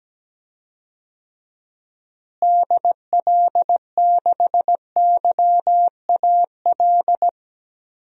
Morse code, heard as a keyed tone, about 17 wpm, 700 Hz. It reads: DL6YAL
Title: DXCC-100 Call Signs - 6 Characters - Repeated 3x - 17wpm